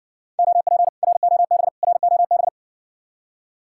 Morse code, heard as a keyed tone, about 58 wpm, 700 Hz. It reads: OP ROB ROB